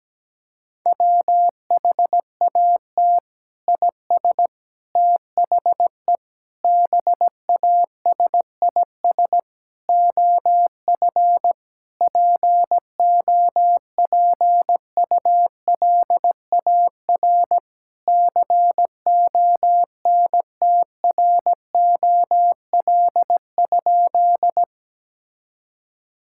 Morse code, heard as a keyed tone, about 17 wpm, 700 Hz. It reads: WHAT IS THE BASIS OF POPULAR CONTROL?
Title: Sentences from Top 1000 Words - Review Part 5 - 17wpm